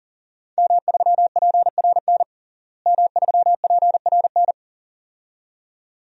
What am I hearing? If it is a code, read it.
Morse code, heard as a keyed tone, about 37 wpm, 700 Hz. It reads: M3PRN M3PRN
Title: DXCC-100 Call Signs - 5 Characters - Repeated 2x - 40wpm